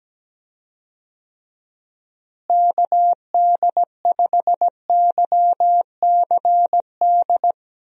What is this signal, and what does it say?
Morse code, heard as a keyed tone, about 17 wpm, 700 Hz. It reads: KD5YCD
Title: USA Call Signs - 6 Characters - Repeated 2x - 17wpm